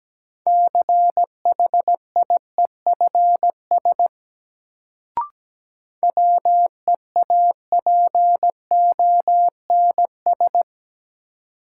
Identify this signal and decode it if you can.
Morse code, heard as a keyed tone, about 17 wpm, 700 Hz. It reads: CHIEFS WEAPONS
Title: Sets of 5 Words Encore - New Word Drill - 17wpm